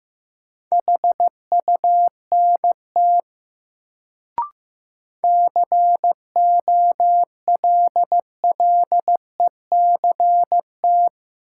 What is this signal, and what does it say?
Morse code, heard as a keyed tone, about 15 wpm, 700 Hz. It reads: HUNT COLLECT
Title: Sentences from Top 1600 Words - New Word Drill - 15wpm